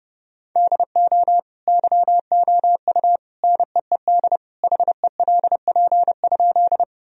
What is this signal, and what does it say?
Morse code, heard as a keyed tone, about 32 wpm, 700 Hz. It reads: DO YOU NEED HELP?
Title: Sentences from Top 1000 Words - Review Part 1 - 30wpm